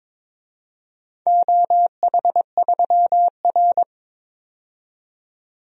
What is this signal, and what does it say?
Morse code, heard as a keyed tone, about 22 wpm, 700 Hz. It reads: OH3R
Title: DXCC-100 Call Signs - 4 Characters - 22wpm